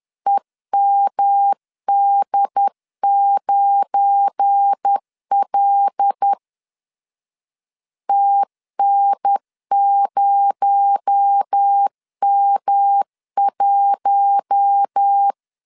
Morse code, heard as a keyed tone, about 10 wpm, 800 Hz. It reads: EMD9L TN0M1
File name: LEZIONI / 8 YJ45 / 8 YJ45 10wpm 2549